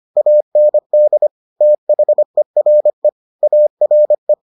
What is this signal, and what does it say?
Morse code, heard as a keyed tone, about 25 wpm, 600 Hz. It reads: AND THERE ARE